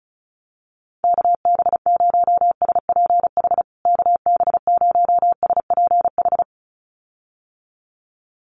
Morse code, heard as a keyed tone, about 34 wpm, 700 Hz. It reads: KB0SPH KB0SPH
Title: USA Call Signs Repeated 2x 35wpm